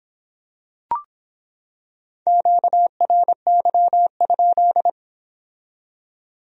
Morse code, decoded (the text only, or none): QRY?